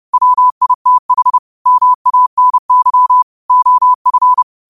Morse code, heard as a keyed tone, about 30 wpm, 1000 Hz. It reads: WITH MANY OF
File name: qotdfiles / 30wpm